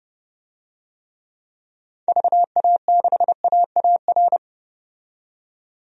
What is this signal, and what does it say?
Morse code, heard as a keyed tone, about 30 wpm, 700 Hz. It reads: VA6AAR
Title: DXCC-100 Call Signs - 6 Characters - 30wpm